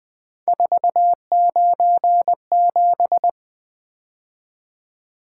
Morse code, heard as a keyed tone, about 20 wpm, 700 Hz. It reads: 497
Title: Three Letter-Number Combinations - 20wpm